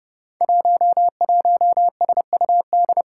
Morse code, heard as a keyed tone, about 28 wpm, 700 Hz. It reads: 11SUD
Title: Five Challenging Letter-Number Combinations 30wpm